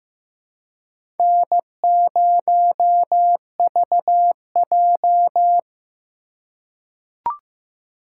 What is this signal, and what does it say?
Morse code, heard as a keyed tone, about 15 wpm, 700 Hz. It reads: N0VJ
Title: USA Call Signs - 4 Characters - 4x Character Spacing - 15wpm